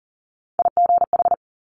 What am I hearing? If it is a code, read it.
Morse code, heard as a keyed tone, about 40 wpm, 700 Hz. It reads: IGH
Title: Top Three Letter Combinations in Words 40wpm